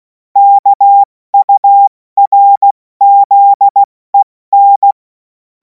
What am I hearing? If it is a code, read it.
Morse code, heard as a keyed tone, about 16 wpm, 800 Hz. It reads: KURZEN
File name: Morsekurs / Witze / Witz 28. April 2026 70 F80